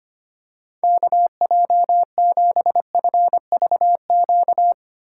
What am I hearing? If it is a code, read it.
Morse code, heard as a keyed tone, about 25 wpm, 700 Hz. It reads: KJ7FVQ
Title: USA Call Signs - 6 Characters - 3x Character Spacing - 25wpm